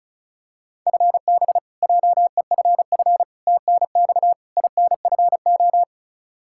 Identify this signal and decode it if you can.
Morse code, heard as a keyed tone, about 36 wpm, 700 Hz. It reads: FB JEFF TNX INFO
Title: Practice QSO - Exchange 3 - More Info - 35wpm